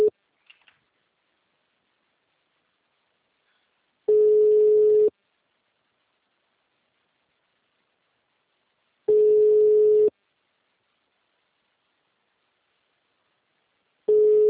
Ringback tone (Europe, 425 Hz).